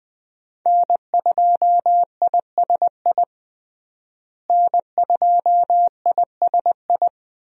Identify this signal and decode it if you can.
Morse code, heard as a keyed tone, about 20 wpm, 700 Hz. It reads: N2ISI N2ISI
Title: USA Call Signs - 5 Characters - Repeated 3x - 20wpm